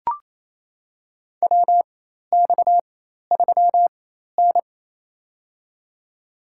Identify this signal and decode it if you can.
Morse code, heard as a keyed tone, about 28 wpm, spaced as for 13 wpm, 700 Hz. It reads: WX3N
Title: USA Call Signs - 4 Characters - 4x Character Spacing - 28wpm